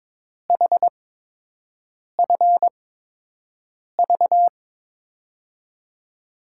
Morse code, heard as a keyed tone, about 22 wpm, spaced as for 6 wpm, 700 Hz. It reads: HFV
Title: USA Call Signs - 8x Character Spacing - 22wpm